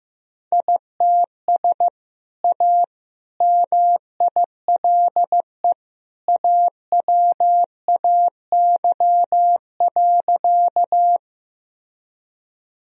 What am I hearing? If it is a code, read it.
Morse code, heard as a keyed tone, about 15 wpm, 700 Hz. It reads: ITS A MILE AWAY.
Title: Sentences from Top 1000 Words - Review Part 1 - 15wpm